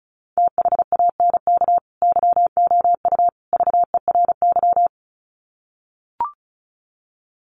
Morse code, heard as a keyed tone, about 34 wpm, 700 Hz. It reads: THANK YOU VERY
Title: Sets of 3 Words 35wpm